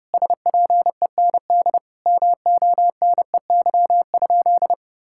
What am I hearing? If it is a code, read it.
Morse code, heard as a keyed tone, about 30 wpm, 700 Hz. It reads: SPEND MONEY?
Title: Sentences from Top 1000 Words - Review Part 5 - 30wpm